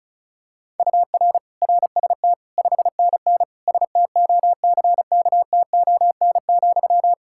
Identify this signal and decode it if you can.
Morse code, heard as a keyed tone, about 36 wpm, 700 Hz. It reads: UR RST 5NN STOCKTON,
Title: K1USN SST QSO Practice 35wpm